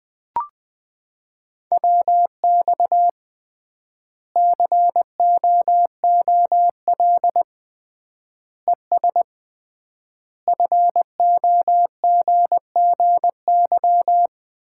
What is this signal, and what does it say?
Morse code, heard as a keyed tone, about 20 wpm, 700 Hz. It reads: WX COOL ES FOGGY
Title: Common QSO Phrases - 3x Word-Spacing - 20wpm